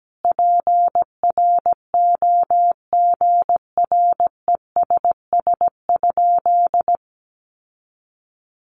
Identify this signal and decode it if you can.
Morse code, heard as a keyed tone, about 17 wpm, 700 Hz. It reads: PROGRESS?
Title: Sentences from Top 1400 Words 17wpm